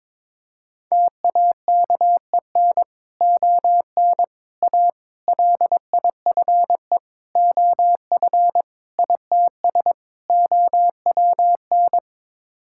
Morse code, heard as a keyed tone, about 22 wpm, 700 Hz. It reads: TAKEN ON A LIFE OF ITS OWN
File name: practice / 22wpm / Sets of 7 Words - 22wpm